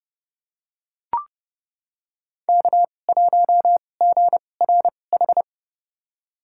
Morse code, heard as a keyed tone, about 30 wpm, spaced as for 22 wpm, 700 Hz. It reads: K1GRH
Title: USA Call Signs - 5 Characters - 2x Character Spacing - 30wpm